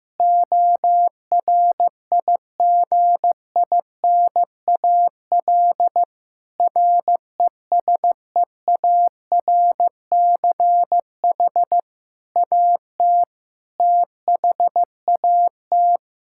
Morse code, heard as a keyed tone, about 15 wpm, 700 Hz. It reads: ORIGINAL RESEARCH AT THAT
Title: Sentences from Top 1000 Words - Review Part 6 - 15wpm